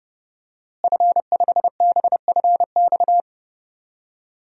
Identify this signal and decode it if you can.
Morse code, heard as a keyed tone, about 30 wpm, 700 Hz. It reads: F5BFX
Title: DXCC-100 Call Signs - 5 Characters - Repeated 3x - 30wpm